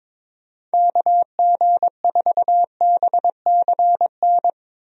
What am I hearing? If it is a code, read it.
Morse code, heard as a keyed tone, about 22 wpm, 700 Hz. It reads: KG4BCN
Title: USA Call Signs - 6 Characters - Repeated 3x - 22wpm